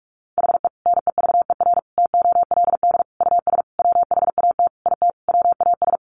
Morse code, heard as a keyed tone, about 44 wpm, 700 Hz. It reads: HE NEVER TOLD US WHAT IT WAS
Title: Sentences from Top 400 Words 45wpm